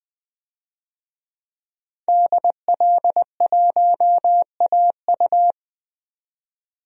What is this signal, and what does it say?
Morse code, heard as a keyed tone, about 20 wpm, 700 Hz. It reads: DL1AU
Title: DXCC-100 Call Signs - 5 Characters - 20wpm